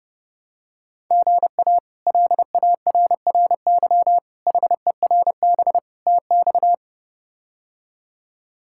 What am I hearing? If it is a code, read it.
Morse code, heard as a keyed tone, about 32 wpm, 700 Hz. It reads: GA LARRY HERB TX
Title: K1USN SST QSO Practice - 30wpm